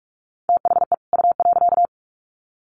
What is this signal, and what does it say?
Morse code, heard as a keyed tone, about 44 wpm, 700 Hz. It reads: THE U.
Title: Sets of 2 Words 45wpm